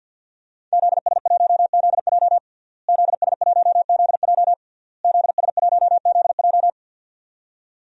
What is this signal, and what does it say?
Morse code, heard as a keyed tone, about 51 wpm, 700 Hz. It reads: ZS1ZJ ZS1ZJ ZS1ZJ
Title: DXCC-100 Call Signs - 5 Characters - Repeated 3x - 50wpm